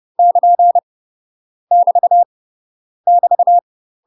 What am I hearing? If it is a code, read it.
Morse code, heard as a keyed tone, about 30 wpm, 700 Hz. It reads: <KN> <BT> <BT>
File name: UK MASTER QSOs TRAINING / 30WPM / UK MASTER QSOs 0021